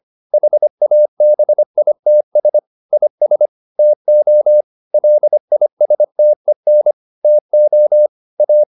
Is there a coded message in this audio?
Yes: Morse code from 0.3 to 8.6 s, heard as a keyed tone, about 25 wpm, 600 Hz. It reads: HABITS IS TO LISTEN TO A